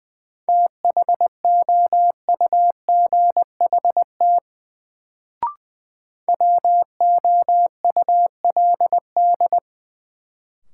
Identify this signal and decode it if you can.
Morse code, heard as a keyed tone, about 20 wpm, 700 Hz. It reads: THOUGHT WOULD
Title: Sets of 2 Words - New Word Drill - 20wpm